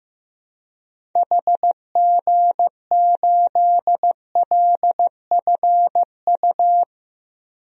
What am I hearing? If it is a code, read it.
Morse code, heard as a keyed tone, about 15 wpm, 700 Hz. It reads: HG8LFU
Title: DXCC-100 Call Signs - 6 Characters - 15wpm